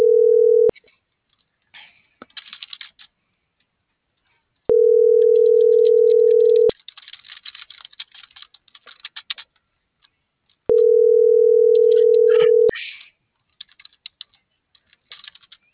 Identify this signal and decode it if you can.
Ringback tone (North America), 440 + 480 Hz.